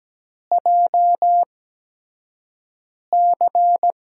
Morse code, heard as a keyed tone, about 17 wpm, spaced as for 5 wpm, 700 Hz. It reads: JC
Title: USA Call Signs - 5 Characters - 8x Character Spacing - 17wpm